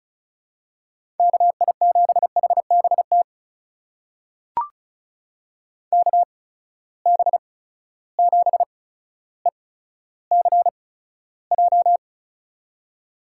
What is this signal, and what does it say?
Morse code, heard as a keyed tone, about 36 wpm, spaced as for 15 wpm, 700 Hz. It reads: KI7HBT KB7ECJ
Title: USA Call Signs - 6 Characters - 8x Character Spacing - 35wpm